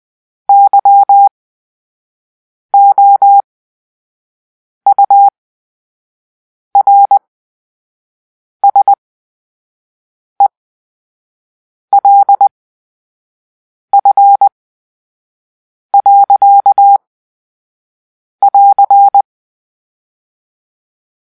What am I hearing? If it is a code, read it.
Morse code, heard as a keyed tone, about 20 wpm, spaced as for 5 wpm, 800 Hz. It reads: YOURSELF. <AR>
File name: Grp-3 Drill-2